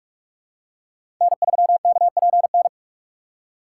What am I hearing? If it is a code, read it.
Morse code, heard as a keyed tone, about 46 wpm, 700 Hz. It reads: N3KPN
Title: USA Call Signs - 5 Characters - 4x Character Spacing - 45wpm